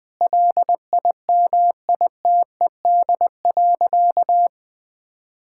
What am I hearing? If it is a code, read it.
Morse code, heard as a keyed tone, about 20 wpm, 700 Hz. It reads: LIMITED.
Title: Sentences from Top 1000 Words - Review Part 6 - 20wpm